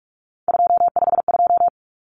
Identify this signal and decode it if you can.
Morse code, heard as a keyed tone, about 44 wpm, 700 Hz. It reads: 252
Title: Numbers - 3-Digit 45wpm